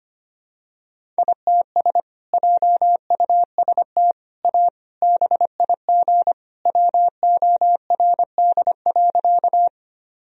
Morse code, heard as a keyed tone, about 25 wpm, 700 Hz. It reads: ITS JUST A BIG WORD.
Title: Sentences from Top 1000 Words - Review Part 1 - 25wpm